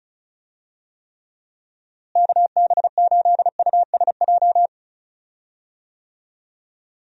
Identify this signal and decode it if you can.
Morse code, heard as a keyed tone, about 36 wpm, 700 Hz. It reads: KB8USJ
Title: USA Call Signs - 6 Characters - Repeated 3x - 35wpm